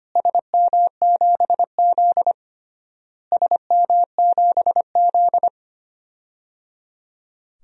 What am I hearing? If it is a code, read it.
Morse code, heard as a keyed tone, about 25 wpm, 700 Hz. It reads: SM7Z SM7Z
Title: DXCC-100 Call Signs - 4 Characters - Repeated 3x - 25wpm